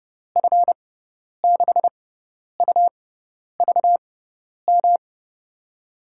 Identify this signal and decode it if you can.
Morse code, heard as a keyed tone, about 32 wpm, spaced as for 10 wpm, 700 Hz. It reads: F6UVM
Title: USA Call Signs - 6x Character Spacing - 30wpm